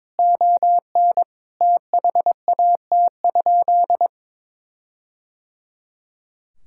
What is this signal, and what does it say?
Morse code, heard as a keyed tone, about 22 wpm, 700 Hz. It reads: ON THAT?
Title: Sentences from Top 1600 Words 22wpm